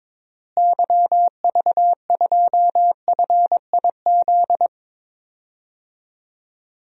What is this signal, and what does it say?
Morse code, heard as a keyed tone, about 22 wpm, 700 Hz. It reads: YV2FIZ
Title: DXCC-100 Call Signs - 6 Characters - Repeated 2x - 22wpm